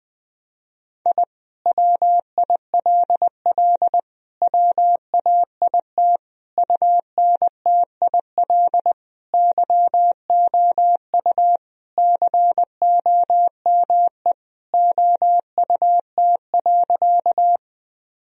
Morse code, heard as a keyed tone, about 21 wpm, 700 Hz. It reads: I WILL WAIT UNTIL YOU COME OUT.
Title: Sentences from Top 1000 Words - Review Part 1 - 20wpm